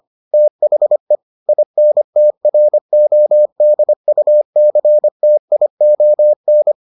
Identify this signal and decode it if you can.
Morse code, heard as a keyed tone, about 25 wpm, 600 Hz. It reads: THE INTRODUCTION